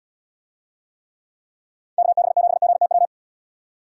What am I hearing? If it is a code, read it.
Morse code, heard as a keyed tone, about 75 wpm, 700 Hz. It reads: KB7GTW